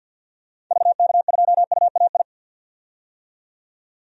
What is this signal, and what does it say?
Morse code, heard as a keyed tone, about 48 wpm, 700 Hz. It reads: VK2UAI
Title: DXCC-100 Call Signs - 6 Characters - 50wpm